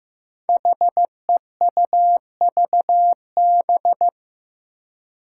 Morse code, heard as a keyed tone, about 15 wpm, 700 Hz. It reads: HEUVB
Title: Five Challenging Letter-Number Combinations 15wpm